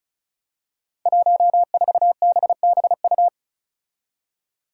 Morse code, heard as a keyed tone, about 34 wpm, 700 Hz. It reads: 14BBU
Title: Five Challenging Letter-Number Combinations 35wpm